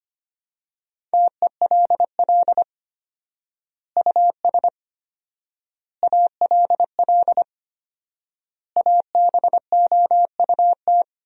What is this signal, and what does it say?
Morse code, heard as a keyed tone, about 25 wpm, 700 Hz. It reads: TELL US ALL ABOUT
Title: 4x Word-Spacing - Sentences from Top 200 Words - 25wpm